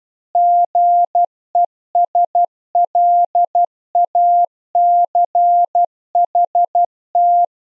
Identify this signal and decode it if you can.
Morse code, heard as a keyed tone, about 12 wpm, 700 Hz. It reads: GESLACHT
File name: Zendamateurnieuws en een stukje geseinde morse